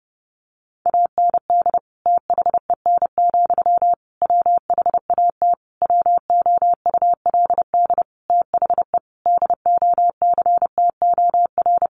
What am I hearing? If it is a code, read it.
Morse code, heard as a keyed tone, about 30 wpm, 700 Hz. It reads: AND THEN, WHAT WOULD THE DOCTOR
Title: Sentences from Top 1500 Words 30wpm